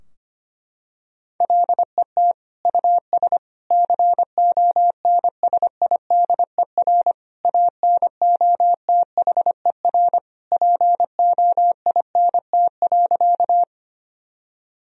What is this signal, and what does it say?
Morse code, heard as a keyed tone, about 25 wpm, 700 Hz. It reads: LET US CONSIDER ANOTHER POINT.
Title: Sentences from Top 1000 Words - Review Part 4 - 25wpm